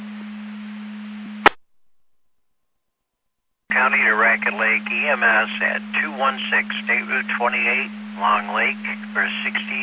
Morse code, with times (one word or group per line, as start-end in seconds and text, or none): none